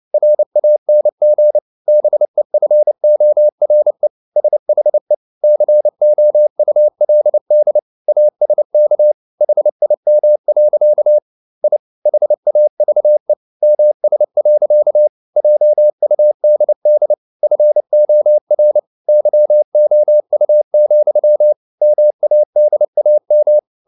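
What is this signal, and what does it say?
Morse code, heard as a keyed tone, about 29 wpm, 600 Hz. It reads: RANG BEFORE SHE COULD ASK HIM. I HAVE MS. JUDD FOR YOU, MADAM
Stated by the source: Morse code practice at 28 wpm, text of Ken Follett's novel Never in English, from CW med Gnister.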